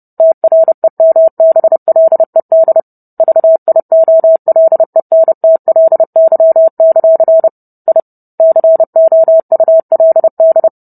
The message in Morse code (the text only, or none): TREMBLED VIOLENTLY; I COULD